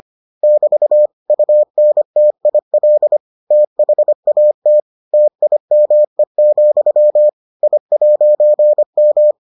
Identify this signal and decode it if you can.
Morse code, heard as a keyed tone, about 25 wpm, 600 Hz. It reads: <BT> UNTIL THAT TIME, I'M